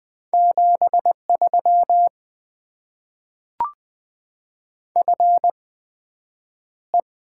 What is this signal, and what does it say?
Morse code, heard as a keyed tone, about 20 wpm, 700 Hz. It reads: 7 3 F E
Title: Top 20 QSO Elements - 8x Character Spacing - 20wpm